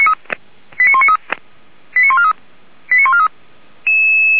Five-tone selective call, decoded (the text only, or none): 89193 89143 89143